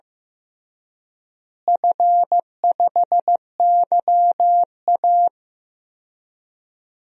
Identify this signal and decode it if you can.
Morse code, heard as a keyed tone, about 15 wpm, 700 Hz. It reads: F5YA
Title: DXCC-100 Call Signs - 4 Characters - Repeated 3x - 15wpm